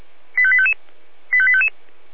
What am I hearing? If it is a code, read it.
Five-tone selective call (ZVEI), heard as quick runs of five tones, tones of about 70 ms. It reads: 85855 85855
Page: /496/SDR/ZVEI/85232/2021.10.21/